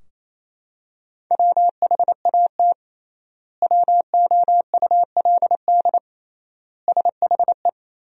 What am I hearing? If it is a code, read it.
Morse code, heard as a keyed tone, about 28 wpm, 700 Hz. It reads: WHAT WOULD SHE
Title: 3x Word-Spacing - Sentences from Top 100 Words - 28wpm